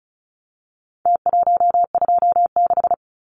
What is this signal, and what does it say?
Morse code, heard as a keyed tone, about 34 wpm, 700 Hz. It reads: T126
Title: Numbers - Contest Style - 4-Digit 35wpm